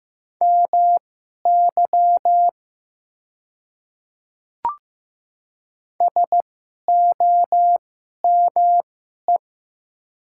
Morse code, heard as a keyed tone, about 15 wpm, spaced as for 11 wpm, 700 Hz. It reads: MY SOME
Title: 2x Character Spacing - Top 200 Words - 15wpm